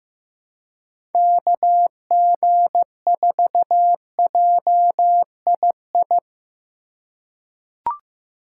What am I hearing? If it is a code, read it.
Morse code, heard as a keyed tone, about 15 wpm, 700 Hz. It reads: KG4JII
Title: USA Call Signs - 6 Characters - 8x Character Spacing - 15wpm